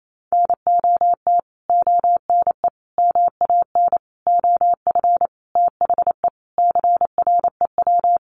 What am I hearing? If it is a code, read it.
Morse code, heard as a keyed tone, about 28 wpm, 700 Hz. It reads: NOT ONE MAN OF THE CREW
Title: Sentences from Top 1500 Words 28wpm